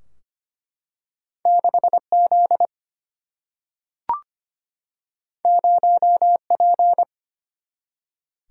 Morse code, heard as a keyed tone, about 25 wpm, 700 Hz. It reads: 6Z 0P
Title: Single-Letter Single-Number 25wpm